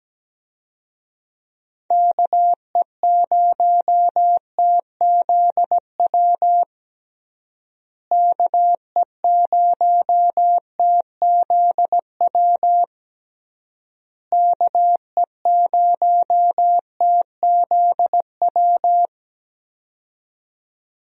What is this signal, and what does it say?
Morse code, heard as a keyed tone, about 17 wpm, 700 Hz. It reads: KE0TZW KE0TZW KE0TZW
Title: USA Call Signs - 6 Characters - Repeated 3x - 17wpm